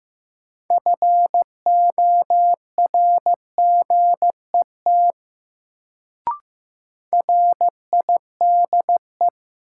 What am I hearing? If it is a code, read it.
Morse code, heard as a keyed tone, about 15 wpm, 700 Hz. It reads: FORGET RIDE
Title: Sentences from Top 1200 Words - New Word Drill Alt - 15wpm